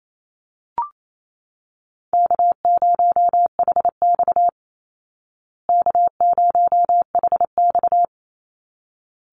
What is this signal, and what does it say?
Morse code, heard as a keyed tone, about 28 wpm, 700 Hz. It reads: K0HX K0HX
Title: Easy Call Signs Repeated 2x - 28wpm